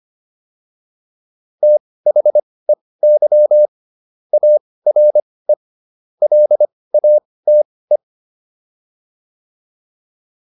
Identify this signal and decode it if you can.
Morse code, heard as a keyed tone, about 25 wpm, spaced as for 18 wpm, 600 Hz. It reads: THEY ARE LATE
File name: PR305 18